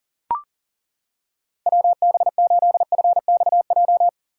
Morse code, heard as a keyed tone, about 37 wpm, 700 Hz. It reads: WB8FXJ
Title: K1USN SST QSO Practice - 40wpm